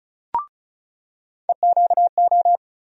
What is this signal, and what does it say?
Morse code, heard as a keyed tone, about 36 wpm, 700 Hz. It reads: EQO
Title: USA Call Sign Postfixes - 3 Letters 35wpm